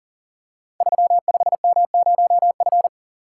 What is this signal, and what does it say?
Morse code, heard as a keyed tone, about 39 wpm, 700 Hz. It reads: 35M0F
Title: Code Groups - 5 Letter:Numbers - 40wpm